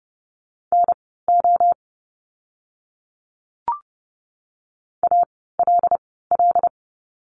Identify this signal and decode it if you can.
Morse code, heard as a keyed tone, about 30 wpm, spaced as for 17 wpm, 700 Hz. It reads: NO ALL
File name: course / 30:10wpm / 230 Lesson 37f - Letter x - Words 30wpm